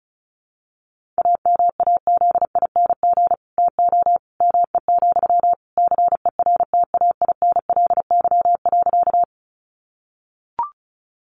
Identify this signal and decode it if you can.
Morse code, heard as a keyed tone, about 34 wpm, 700 Hz. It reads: AMAZING TO ME, CERTAINLY.
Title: Sentences from Top 1500 Words 35wpm